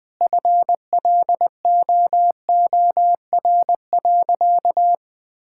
Morse code, heard as a keyed tone, about 20 wpm, 700 Hz. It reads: FLOOR.
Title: Sentences with 3-Word Phrases - Spoken Beginning - 20wpm